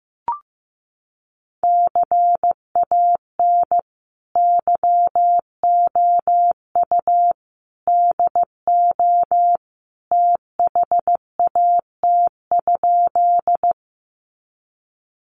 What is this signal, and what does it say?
Morse code, heard as a keyed tone, about 15 wpm, 700 Hz. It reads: CAN YOU DO THAT?
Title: Sentences from Top 100 Words 15wpm